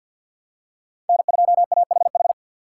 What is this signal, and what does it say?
Morse code, heard as a keyed tone, about 49 wpm, 700 Hz. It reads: N2AHH
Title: USA Call Signs - 2x Character Spacing - 50wpm